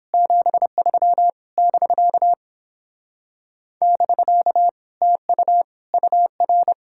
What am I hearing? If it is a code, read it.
Morse code, heard as a keyed tone, about 29 wpm, 700 Hz. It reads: 73 <BK> <BK> TU UR